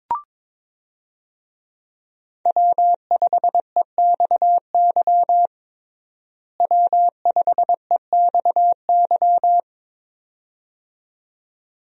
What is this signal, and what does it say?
Morse code, heard as a keyed tone, about 22 wpm, 700 Hz. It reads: W5EXY W5EXY
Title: USA Call Signs - 5 Characters - Repeated 2x - 22wpm